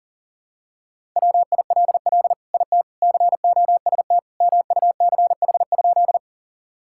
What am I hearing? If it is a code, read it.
Morse code, heard as a keyed tone, about 40 wpm, 700 Hz. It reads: WILL IT COST MUCH?
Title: Sentences from Top 1000 Words - Review Part 5 - 40wpm